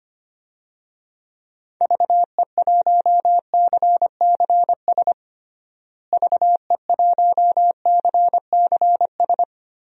Morse code, heard as a keyed tone, about 25 wpm, 700 Hz. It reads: VE1CCS VE1CCS
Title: DXCC-100 Call Signs - 6 Characters - Repeated 2x - 25wpm